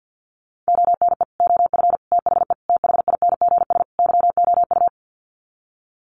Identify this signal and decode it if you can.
Morse code, heard as a keyed tone, about 48 wpm, 700 Hz. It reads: ONE OF THE THINGS YOU
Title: Sets of 5 Words Encore2 50wpm